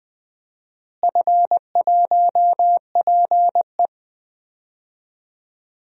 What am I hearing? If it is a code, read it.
Morse code, heard as a keyed tone, about 20 wpm, 700 Hz. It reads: F1PE
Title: DXCC-100 Call Signs - 4 Characters - Repeated 2x - 20wpm